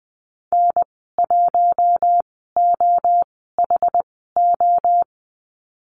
Morse code, heard as a keyed tone, about 20 wpm, spaced as for 14 wpm, 700 Hz. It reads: N1OHO
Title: 062 Lesson 11g - Letter h - Call Signs 20wpm